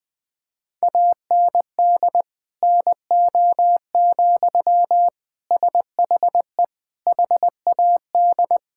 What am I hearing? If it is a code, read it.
Morse code, heard as a keyed tone, about 20 wpm, 700 Hz. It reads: AND NO, SHE HAD
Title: Sentences from Top 1600 Words 20wpm